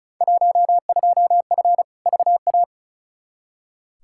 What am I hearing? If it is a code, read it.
Morse code, heard as a keyed tone, about 36 wpm, 700 Hz. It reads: 12F VA
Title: Field Day Exchanges Encore - 35wpm